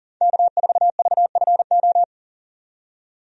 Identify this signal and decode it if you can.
Morse code, heard as a keyed tone, about 37 wpm, 700 Hz. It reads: K4VFO
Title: USA Call Signs - 5 Characters - 40wpm